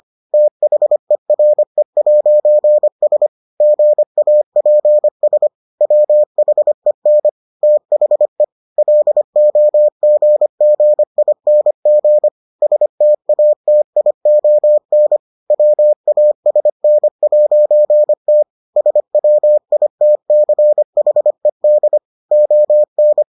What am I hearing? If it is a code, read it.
Morse code, heard as a keyed tone, about 25 wpm, 600 Hz. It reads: THERE'S GAPS WHEN THE LOGGING STATION WASN'T SWITCHED ON